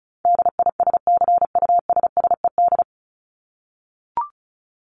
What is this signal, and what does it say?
Morse code, heard as a keyed tone, about 34 wpm, 700 Hz. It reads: DISCUSSED
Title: ED Suffix - Top 100 Words with Base Words 35wpm